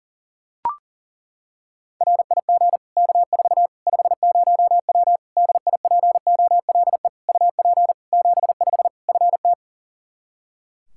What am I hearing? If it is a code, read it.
Morse code, heard as a keyed tone, about 40 wpm, 700 Hz. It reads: RIG K4 50W DIPOLE UP 75 FT
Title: Practice QSO - Exchange 2 - Info - 40wpm